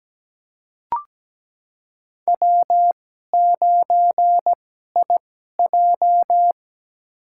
Morse code, heard as a keyed tone, about 17 wpm, spaced as for 12 wpm, 700 Hz. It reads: W9IJ
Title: USA Call Signs - 4 Characters - 2x Character Spacing - 17wpm